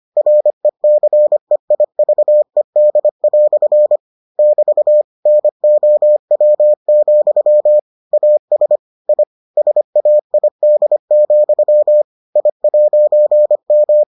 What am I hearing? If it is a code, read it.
Morse code, heard as a keyed tone, about 25 wpm, 600 Hz. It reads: RECEIVED" <BT> NOW, AS I SAID, I'M